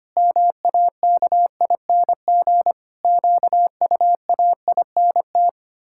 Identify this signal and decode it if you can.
Morse code, heard as a keyed tone, about 25 wpm, 700 Hz. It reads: MAKING QUAINT
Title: Pangram Challenge v2 - 25wpm